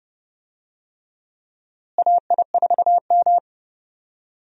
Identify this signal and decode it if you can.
Morse code, heard as a keyed tone, about 30 wpm, 700 Hz. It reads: AI4M
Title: USA Call Signs - 4 Characters - Repeated 3x - 30wpm